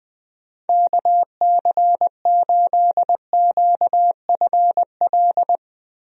Morse code, heard as a keyed tone, about 20 wpm, 700 Hz. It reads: KC8QFL
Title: USA Call Signs - 6 Characters - Repeated 3x - 20wpm